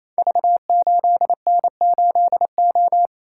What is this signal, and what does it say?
Morse code, heard as a keyed tone, about 28 wpm, 700 Hz. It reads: V8N8O
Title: Code Groups - 5 Letter:Numbers - 28wpm